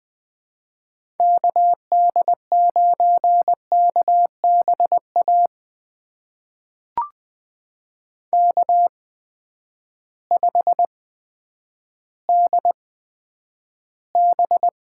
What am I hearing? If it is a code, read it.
Morse code, heard as a keyed tone, about 20 wpm, spaced as for 10 wpm, 700 Hz. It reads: KD9KBA K5DB
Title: USA Call Signs - 8x Character Spacing - 20wpm